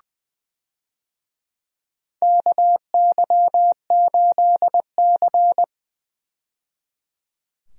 Morse code, heard as a keyed tone, about 20 wpm, 700 Hz. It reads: KY8C
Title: USA Call Signs - 4 Characters - 20wpm